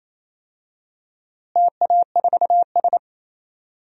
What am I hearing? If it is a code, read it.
Morse code, heard as a keyed tone, about 28 wpm, 700 Hz. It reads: TA4S
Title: DXCC-100 Call Signs - 4 Characters - Repeated 3x - 28wpm